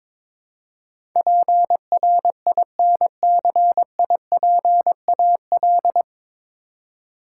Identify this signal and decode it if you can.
Morse code, heard as a keyed tone, about 22 wpm, 700 Hz. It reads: PRINCIPAL
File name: 9 Letter Words - First 3 Letters Given - 22wpm